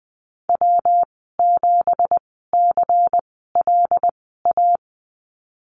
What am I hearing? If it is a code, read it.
Morse code, heard as a keyed tone, about 20 wpm, spaced as for 14 wpm, 700 Hz. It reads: W7CLA
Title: 179 Lesson 29g - Slash - Call Signs 20wpm